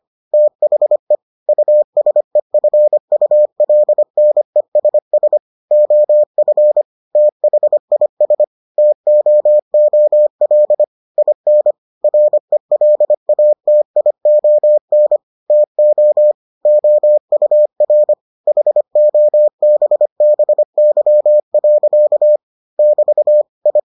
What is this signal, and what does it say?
Morse code, heard as a keyed tone, about 25 wpm, 600 Hz. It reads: THE USEFULNESS OF THIS TOOL IN RELATION TO OUR HOBBY. <BT> I